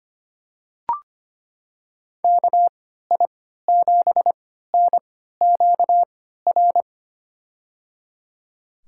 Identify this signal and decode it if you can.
Morse code, heard as a keyed tone, about 25 wpm, spaced as for 14 wpm, 700 Hz. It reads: KI7NQR
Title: USA Call Signs - 6 Characters - 3x Character Spacing - 25wpm